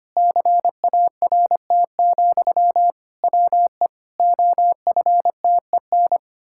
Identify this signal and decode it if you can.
Morse code, heard as a keyed tone, about 25 wpm, 700 Hz. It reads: CART, WE OFTEN
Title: Pangram Challenge v2 - 25wpm